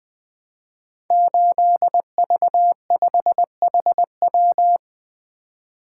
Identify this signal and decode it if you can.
Morse code, heard as a keyed tone, about 20 wpm, 700 Hz. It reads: 8V5HW
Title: Code Groups - 5 Letter:Numbers - 20wpm